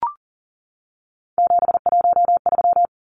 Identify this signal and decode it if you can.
Morse code, heard as a keyed tone, about 40 wpm, 700 Hz. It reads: 713
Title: Numbers - 3-Digit 40wpm